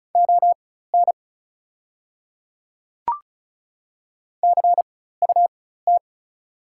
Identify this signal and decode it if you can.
Morse code, heard as a keyed tone, about 36 wpm, spaced as for 16 wpm, 690 Hz. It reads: ON CUT
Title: 4x Character Spacing - Top 500 Words - 35wpm